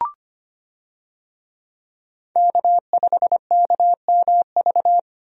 Morse code, heard as a keyed tone, about 25 wpm, 700 Hz. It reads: K5KMV
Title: USA Call Signs - 5 Characters - 25wpm